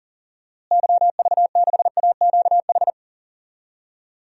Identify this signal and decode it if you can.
Morse code, heard as a keyed tone, about 40 wpm, 700 Hz. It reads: YV6AQH
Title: DXCC-100 Call Signs - 6 Characters - Repeated 2x - 40wpm